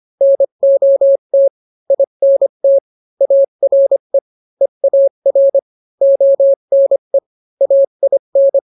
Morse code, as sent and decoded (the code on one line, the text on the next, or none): -. --- - / .. -. - / .- .-. / . .- .-. / --- -. / .- .. -.
NOT INT ARE EAR ONE AIN